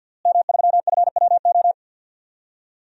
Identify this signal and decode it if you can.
Morse code, heard as a keyed tone, about 51 wpm, 700 Hz. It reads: M3FWO
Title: DXCC-100 Call Signs - 5 Characters - Repeated 3x - 50wpm